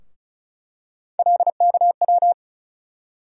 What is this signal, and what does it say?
Morse code, heard as a keyed tone, about 36 wpm, 700 Hz. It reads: LKW
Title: USA Call Sign Postfixes - 3 Letters 35wpm